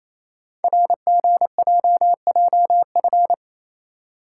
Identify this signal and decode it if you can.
Morse code, heard as a keyed tone, about 27 wpm, 700 Hz. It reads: RGJJF
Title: Code Groups - 5 Letters - 28wpm